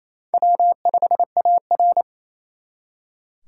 Morse code, heard as a keyed tone, about 35 wpm, 700 Hz. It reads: W5AR
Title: Speed Racing - USA 1x2 Call Signs 28wpm